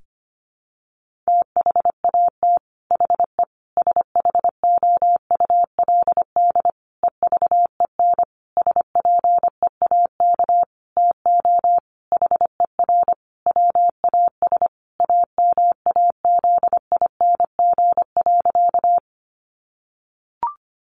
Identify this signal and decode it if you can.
Morse code, heard as a keyed tone, about 25 wpm, 700 Hz. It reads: THAT HE SHOULD EVEN SPEAK TO HER WAS AMAZING.
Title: Sentences from Top 1500 Words 25wpm